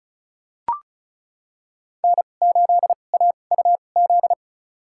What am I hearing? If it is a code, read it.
Morse code, heard as a keyed tone, about 36 wpm, spaced as for 25 wpm, 700 Hz. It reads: N8AUZ
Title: USA Call Signs - 5 Characters - 2x Character Spacing - 35wpm